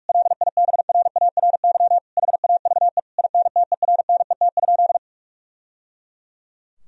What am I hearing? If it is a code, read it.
Morse code, heard as a keyed tone, about 46 wpm, 700 Hz. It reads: LIBRARY HAVE INTERNET?
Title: Sentences from Top 1000 Words - Review Part 4 - 45wpm